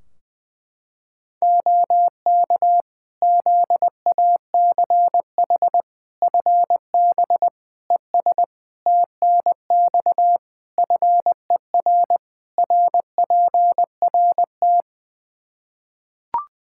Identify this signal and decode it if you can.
Morse code, heard as a keyed tone, about 20 wpm, 700 Hz. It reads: OK ZACH FB ES TNX FER RPRT
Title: K1USN SST QSO Practice 20wpm